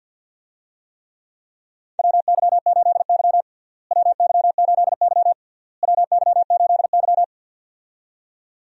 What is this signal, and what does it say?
Morse code, heard as a keyed tone, about 48 wpm, 700 Hz. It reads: WY8Y WY8Y WY8Y
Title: USA Call Signs - 4 Characters - Repeated 3x - 50wpm